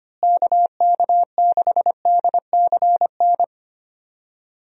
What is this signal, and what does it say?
Morse code, heard as a keyed tone, about 25 wpm, 700 Hz. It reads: KK6DCN
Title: USA Call Signs - 6 Characters - Repeated 3x - 25wpm